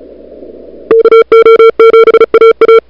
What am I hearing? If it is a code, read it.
Morse code, heard as a keyed tone, about 34 wpm, 440 Hz. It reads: KO7AA
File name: n6tr / Fab2007CWSprintHour1